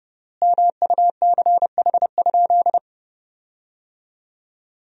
Morse code, heard as a keyed tone, about 32 wpm, 700 Hz. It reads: MUCH?
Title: 3x Word-Spacing - Sentences from Top 400 Words - 30wpm